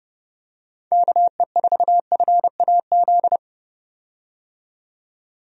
Morse code, heard as a keyed tone, about 32 wpm, 700 Hz. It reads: KE4FAZ
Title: USA Call Signs - 6 Characters - 30wpm